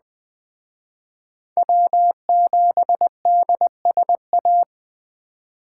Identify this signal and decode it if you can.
Morse code, heard as a keyed tone, about 20 wpm, 700 Hz. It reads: W7DSA
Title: Speed Racing - USA 1x3 Call Signs 17wpm